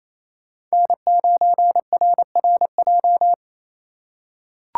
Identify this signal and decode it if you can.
Morse code, heard as a keyed tone, about 28 wpm, 700 Hz. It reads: N9RRJ
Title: USA Call Signs - 6x Character Spacing - 28wpm